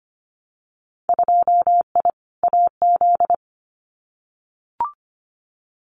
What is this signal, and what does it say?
Morse code, heard as a keyed tone, about 25 wpm, 700 Hz. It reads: 2I AZ
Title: Winter Field Day Exchanges Encore 25wpm